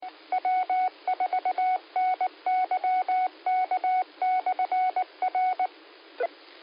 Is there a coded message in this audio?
Yes: Morse code from 0.3 to 5.7 s, heard as a keyed tone, about 19 wpm, 740 Hz. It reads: W4NYK/R